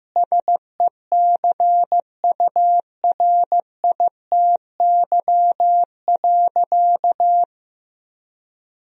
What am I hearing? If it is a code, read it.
Morse code, heard as a keyed tone, about 15 wpm, 700 Hz. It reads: SECURITY.
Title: Sentences from Top 1000 Words - Review Part 4 - 15wpm